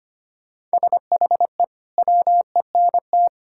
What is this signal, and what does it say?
Morse code, heard as a keyed tone, about 25 wpm, 700 Hz. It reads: SHE WENT
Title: Sentences from Top 1600 Words 25wpm